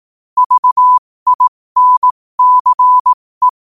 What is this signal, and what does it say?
Morse code, heard as a keyed tone, about 18 wpm, spaced as for 15 wpm, 1000 Hz. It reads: VINCE
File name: qotdfiles / 15wpm